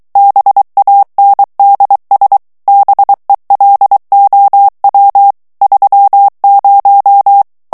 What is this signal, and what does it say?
Morse code, heard as a keyed tone, about 23 wpm, 800 Hz. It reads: BANDS BELOW 30
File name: ANGLAIS texte clair 3 vitesse 22 mots minute